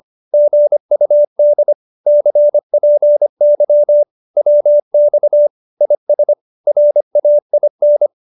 Morse code, heard as a keyed tone, about 25 wpm, 590 Hz. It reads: GUD CPY WX IS RAIN
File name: QSO305 25